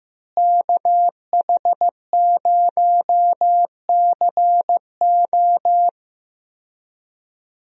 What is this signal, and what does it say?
Morse code, heard as a keyed tone, about 15 wpm, 700 Hz. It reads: KH0CO
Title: USA Call Signs - 5 Characters - Repeated 2x - 15wpm